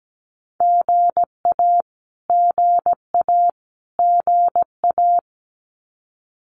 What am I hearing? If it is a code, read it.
Morse code, heard as a keyed tone, about 17 wpm, 700 Hz. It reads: GA GA GA
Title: Abbreviations Repeated 3x - 17wpm